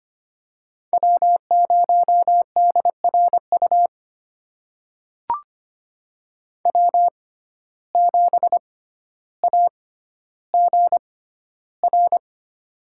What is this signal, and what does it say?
Morse code, heard as a keyed tone, about 25 wpm, spaced as for 13 wpm, 700 Hz. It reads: W0DRU W7AGR
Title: USA Call Signs - 5 Characters - 6x Character Spacing - 25wpm